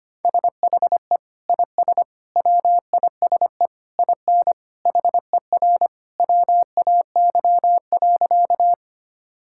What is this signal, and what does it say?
Morse code, heard as a keyed tone, about 25 wpm, 700 Hz. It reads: SHE IS WISE IN HER WAY.
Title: Sentences from Top 1600 Words 25wpm